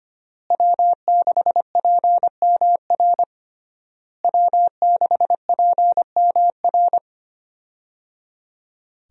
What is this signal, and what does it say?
Morse code, heard as a keyed tone, about 25 wpm, 700 Hz. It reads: W6PMR W6PMR
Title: USA Call Signs - 5 Characters - Repeated 3x - 25wpm